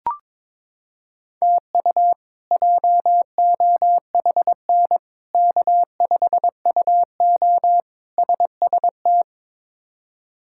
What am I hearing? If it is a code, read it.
Morse code, heard as a keyed tone, about 22 wpm, 700 Hz. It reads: TU JOHN K5UO SST
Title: K1USN SST QSO Practice - 22wpm